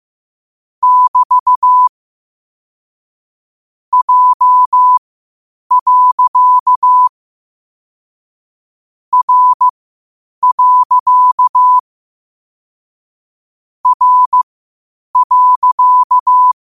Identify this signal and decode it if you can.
Morse code, heard as a keyed tone, about 15 wpm, spaced as for 8 wpm, 1000 Hz. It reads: <BT> J. R. R.
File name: qotdfiles / 8wpm